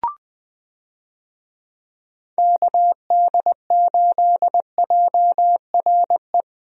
Morse code, heard as a keyed tone, about 20 wpm, 700 Hz. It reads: KD8JRE
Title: USA Call Signs - 6 Characters - Repeated 3x - 20wpm